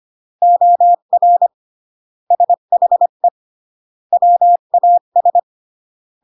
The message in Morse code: OR SHE WAS